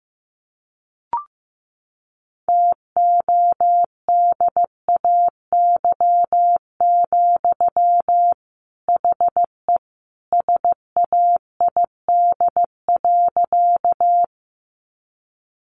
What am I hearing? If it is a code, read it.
Morse code, heard as a keyed tone, about 15 wpm, 700 Hz. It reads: TODAY, HE SAID.
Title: Sentences from Top 600 Words 15wpm